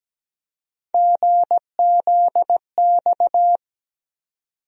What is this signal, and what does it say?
Morse code, heard as a keyed tone, about 17 wpm, 700 Hz. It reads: GZX
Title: USA Call Sign Postfixes - 3 Letters 17wpm